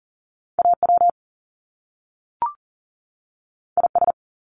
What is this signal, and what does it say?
Morse code, heard as a keyed tone, about 42 wpm, 700 Hz. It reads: AW IS
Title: Top Two Letter Combinations in Words 40wpm